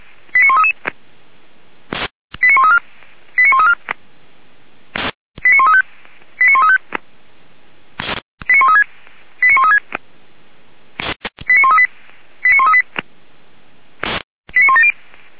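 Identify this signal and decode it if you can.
Five-tone selective call (ZVEI), heard as quick runs of five tones, tones of about 80 ms. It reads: 89133 89134 89134 89135 89135 89136 89136 89138 89138 89170
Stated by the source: Five-tone ZVEI selective call received on SDR, from the Feuerwehr Orlamünde (fire brigade) audio server.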